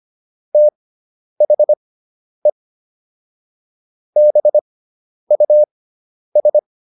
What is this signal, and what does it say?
Morse code, heard as a keyed tone, about 25 wpm, spaced as for 10 wpm, 600 Hz. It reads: THE BUS